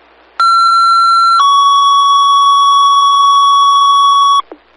Two-tone page, heard as a one-second tone, then a three-second tone, about 1360 then 1120 Hz. It is heard from 0.4 to 4.4 s.